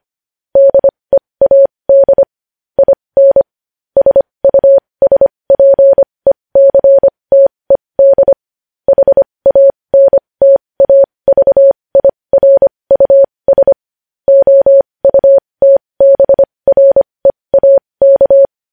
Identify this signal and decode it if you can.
Morse code, heard as a keyed tone, about 25 wpm, spaced as for 20 wpm, 550 Hz. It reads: DEAD IN SUSPECTED HANTAVIRUS OUTBREAK
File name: podcast / News headlines-1x-25x20